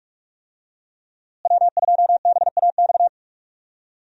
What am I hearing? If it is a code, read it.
Morse code, heard as a keyed tone, about 46 wpm, 700 Hz. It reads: W2BAX
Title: USA Call Signs - 5 Characters - Repeated 2x - 45wpm